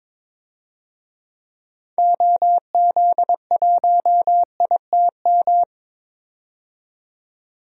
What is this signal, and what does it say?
Morse code, heard as a keyed tone, about 22 wpm, 700 Hz. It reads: OZ1ITM
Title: DXCC-100 Call Signs - 6 Characters - 22wpm